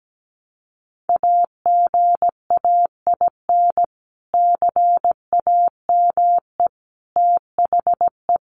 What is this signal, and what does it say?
Morse code, heard as a keyed tone, about 17 wpm, 700 Hz. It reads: AGAIN CAME THE
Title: Sentences from Top 500 Words 17wpm